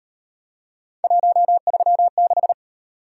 Morse code, heard as a keyed tone, about 38 wpm, 700 Hz. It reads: 136